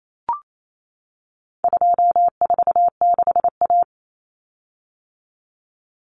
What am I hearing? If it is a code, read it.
Morse code, heard as a keyed tone, about 28 wpm, 700 Hz. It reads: 246A
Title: Common Cut Numbers - 4-Digit 28wpm